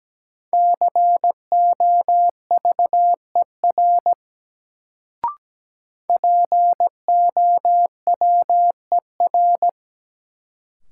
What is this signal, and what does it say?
Morse code, heard as a keyed tone, about 17 wpm, 700 Hz. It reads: COVER POWER
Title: Sentences from Top 500 Words - New Word Drill - 17wpm